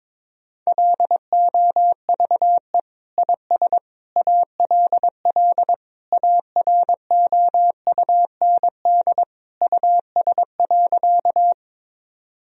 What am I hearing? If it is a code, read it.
Morse code, heard as a keyed tone, about 22 wpm, 700 Hz. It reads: LOVE IS ALL AROUND US.
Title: Sentences from Top 1000 Words - Review Part 3 - 22wpm